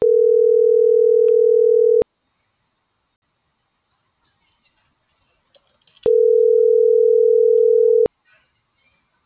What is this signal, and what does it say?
Ringback tone (North America), 440 + 480 Hz.